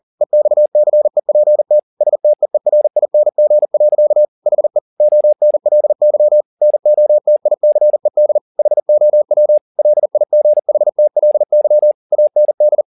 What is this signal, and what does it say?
Morse code, heard as a keyed tone, about 40 wpm, 600 Hz. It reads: EXCEPT STEERING. HE ONLY NOTICED HOW LIGHTLY AND
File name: the old man and the sea 2 40wpm